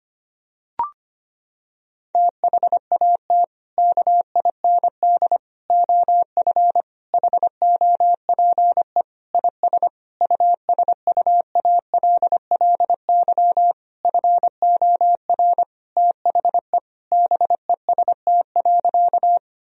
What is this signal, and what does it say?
Morse code, heard as a keyed tone, about 25 wpm, 700 Hz. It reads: THAT KIND OF HOPE IS USUALLY FOR THE BEST.
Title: Sentences from Top 1000 Words - Review Part 3 - 25wpm